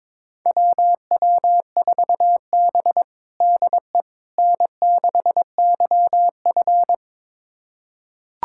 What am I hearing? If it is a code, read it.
Morse code, heard as a keyed tone, about 22 wpm, 700 Hz. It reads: WW4B DE N6YF
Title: USA Callsign DE Callsign - 4 Characters - 22wpm